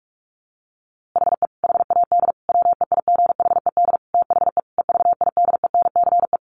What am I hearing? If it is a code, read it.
Morse code, heard as a keyed tone, about 44 wpm, 700 Hz. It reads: HE HAD WEIGHED THE EVIDENCE
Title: A Sound Of Thunder 45wpm